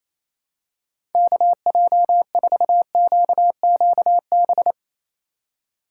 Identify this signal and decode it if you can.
Morse code, heard as a keyed tone, about 28 wpm, 700 Hz. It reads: KJ4QQB
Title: USA Call Signs - 6 Characters - 2x Character Spacing - 28wpm